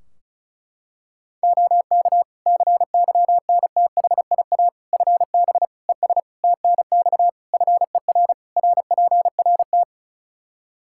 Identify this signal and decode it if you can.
Morse code, heard as a keyed tone, about 36 wpm, 700 Hz. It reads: OK CYNTHIA FB ES TNX FER RPRT
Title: K1USN SST QSO Practice 35wpm